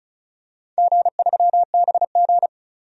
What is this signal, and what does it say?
Morse code, heard as a keyed tone, about 34 wpm, 700 Hz. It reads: G3BG
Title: DXCC-100 Call Signs - 4 Characters - Repeated 2x - 35wpm